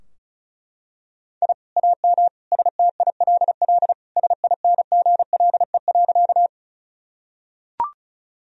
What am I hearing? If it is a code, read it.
Morse code, heard as a keyed tone, about 36 wpm, 700 Hz. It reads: I AM STILL SINGLE.
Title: Sentences from Top 1000 Words - Review Part 4 - 35wpm